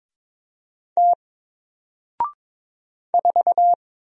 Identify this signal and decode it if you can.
Morse code, heard as a keyed tone, about 22 wpm, 700 Hz. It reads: T 4
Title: Single Character (Letter:Number:Punctuation:Prosign) Alt 22wpm